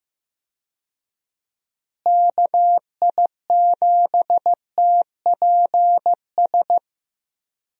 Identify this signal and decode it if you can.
Morse code, heard as a keyed tone, about 15 wpm, 700 Hz. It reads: KI7TPS
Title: USA Call Signs - 6 Characters - Repeated 2x - 15wpm